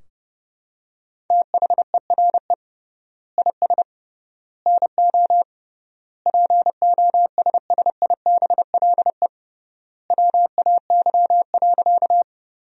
Morse code, heard as a keyed tone, about 32 wpm, 700 Hz. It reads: THERE IS NO POSSIBLE WAY.
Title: 3x Word-Spacing - Sentences from Top 500 Words - 30wpm